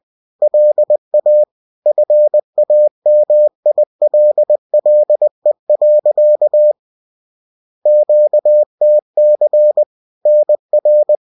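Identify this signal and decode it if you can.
Morse code, heard as a keyed tone, about 20 wpm, 600 Hz. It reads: LA FAMILLE. QTC NR
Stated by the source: Bulletin sent in CW.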